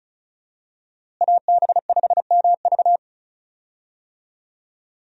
Morse code, heard as a keyed tone, about 36 wpm, 700 Hz. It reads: AB5MV
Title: USA Call Signs - 5 Characters - Repeated 3x - 35wpm